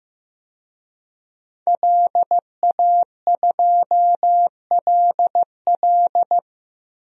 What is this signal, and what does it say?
Morse code, heard as a keyed tone, about 16 wpm, 700 Hz. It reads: LA2LL
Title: DXCC-100 Call Signs - 5 Characters - Repeated 2x - 15wpm